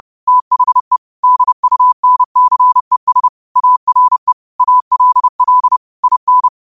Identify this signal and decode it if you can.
Morse code, heard as a keyed tone, about 30 wpm, 1000 Hz. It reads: THE DUNCES ARE ALL IN